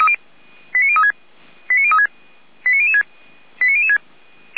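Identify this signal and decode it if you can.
Five-tone selective call (ZVEI), heard as quick runs of five tones, tones of about 70 ms. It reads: 89036 89036 89006 89006
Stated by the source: ZVEI five-tone selective call received on SDR.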